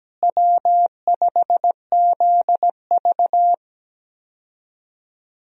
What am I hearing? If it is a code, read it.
Morse code, heard as a keyed tone, about 17 wpm, 700 Hz. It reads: W5ZV
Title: USA Call Signs - 4 Characters - Repeated 2x - 17wpm